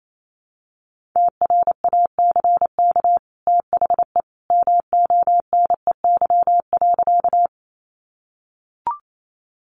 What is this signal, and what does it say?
Morse code, heard as a keyed tone, about 28 wpm, 700 Hz. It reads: TRACK THE MONEY.
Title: Sentences from Top 1100 Words 28wpm